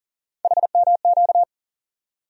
Morse code, heard as a keyed tone, about 40 wpm, 700 Hz. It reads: HMQ
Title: USA Call Sign Postfixes - 3 Letters 40wpm